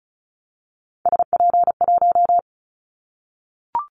Morse code, heard as a keyed tone, about 36 wpm, 700 Hz. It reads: SP1
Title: Callsign Prefixes for Top 25 DXCC - 35wpm